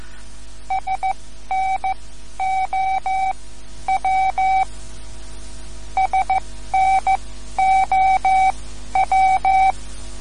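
Morse code, heard as a keyed tone, about 15 wpm, spaced as for 11 wpm, 760 Hz. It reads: SNOW SNOW
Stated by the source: Morse code lesson, third lesson in a series for learning the Morse alphabet.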